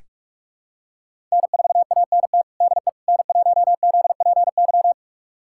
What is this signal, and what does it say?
Morse code, heard as a keyed tone, about 46 wpm, 700 Hz. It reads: N4ANT DE N1ZPY
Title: USA Callsign DE Callsign - 5 Characters - 45wpm